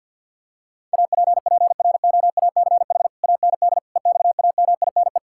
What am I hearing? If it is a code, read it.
Morse code, heard as a keyed tone, about 52 wpm, 700 Hz. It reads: APPROACH AND EXAMINE
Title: Sentences from Top 1600 Words 50wpm